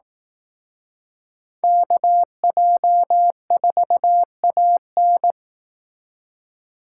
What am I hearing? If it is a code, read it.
Morse code, heard as a keyed tone, about 19 wpm, 700 Hz. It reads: KJ4AN